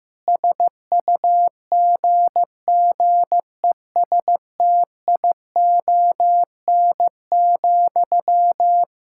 Morse code, heard as a keyed tone, about 15 wpm, 700 Hz. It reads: SUGGESTION,
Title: Sentences from Top 1600 Words 15wpm